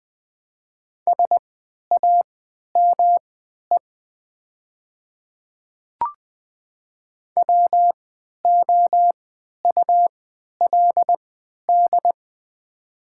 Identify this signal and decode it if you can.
Morse code, heard as a keyed tone, about 20 wpm, spaced as for 11 wpm, 700 Hz. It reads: SAME WOULD
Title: 3x Character Spacing - Top 200 Words - 20wpm